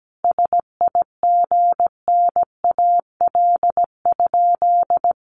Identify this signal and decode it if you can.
Morse code, heard as a keyed tone, about 17 wpm, 700 Hz. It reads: SIGNAL?
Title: Sentences from Top 1400 Words 17wpm